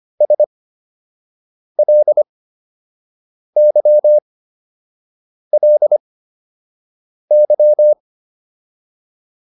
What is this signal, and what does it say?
Morse code, heard as a keyed tone, about 25 wpm, spaced as for 6 wpm, 600 Hz. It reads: SLYLY